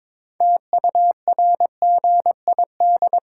Morse code, heard as a keyed tone, about 22 wpm, 700 Hz. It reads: TURGID
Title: Pangram Challenge v2 - 22wpm